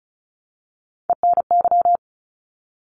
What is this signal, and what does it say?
Morse code, heard as a keyed tone, about 34 wpm, 700 Hz. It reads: ENY
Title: ARRL Sections 35wpm